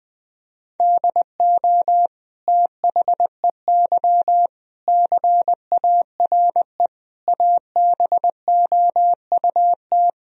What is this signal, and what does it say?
Morse code, heard as a keyed tone, about 20 wpm, 700 Hz. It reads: DO THEY CARE ABOUT
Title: Sentences from Top 1000 Words - Review Part 3 - 20wpm